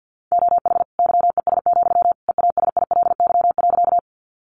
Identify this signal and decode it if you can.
Morse code, heard as a keyed tone, about 50 wpm, 700 Hz. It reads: OH YES, EASILY.
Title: Sentences from Top 900 Words 50wpm